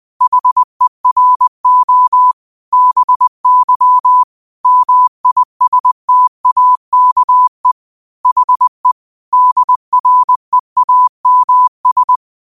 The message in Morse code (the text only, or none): HERO BY MISTAKE HE DREAMS